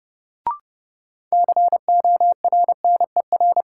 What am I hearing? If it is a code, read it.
Morse code, heard as a keyed tone, about 30 wpm, 700 Hz. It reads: CORNER
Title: Sets of 5 Words - New Word Drill - 30wpm